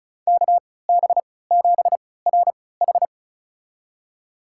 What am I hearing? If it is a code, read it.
Morse code, heard as a keyed tone, about 34 wpm, spaced as for 20 wpm, 700 Hz. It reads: KB7RH
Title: USA Call Signs - 3x Character Spacing - 35wpm